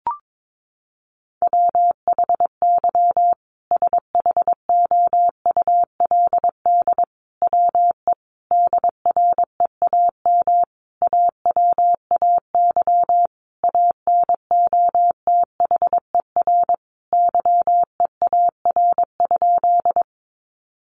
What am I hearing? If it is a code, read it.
Morse code, heard as a keyed tone, about 22 wpm, 700 Hz. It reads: WHY SHOULD WE DREAM AWAY ANOTHER YEAR?
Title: Sentences from Top 1500 Words 22wpm